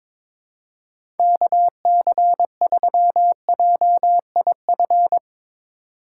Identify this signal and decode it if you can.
Morse code, heard as a keyed tone, about 21 wpm, 700 Hz. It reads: KC3JIF
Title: USA Call Signs - 6 Characters - 2x Character Spacing - 22wpm